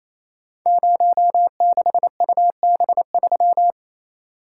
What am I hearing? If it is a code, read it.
Morse code, heard as a keyed tone, about 28 wpm, 700 Hz. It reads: 06UB3
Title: Code Groups - 5 Letter:Numbers - 28wpm